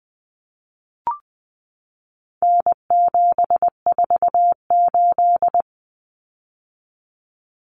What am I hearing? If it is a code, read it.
Morse code, heard as a keyed tone, about 20 wpm, 700 Hz. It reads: N748
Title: Common Cut Numbers - 4-Digit 20wpm